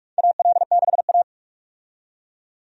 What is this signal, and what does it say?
Morse code, heard as a keyed tone, about 46 wpm, 700 Hz. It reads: AL6A
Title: USA Call Signs - 4 Characters - 45wpm